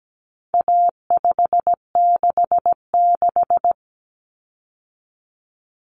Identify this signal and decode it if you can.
Morse code, heard as a keyed tone, about 17 wpm, 700 Hz. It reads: A566
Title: Common Cut Numbers - 4-Digit 17wpm